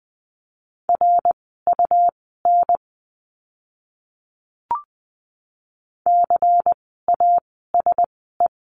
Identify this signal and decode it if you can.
Morse code, heard as a keyed tone, about 20 wpm, spaced as for 15 wpm, 700 Hz. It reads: RUN CASE
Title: 101 Lesson 17f - Letter u - Words 20wpm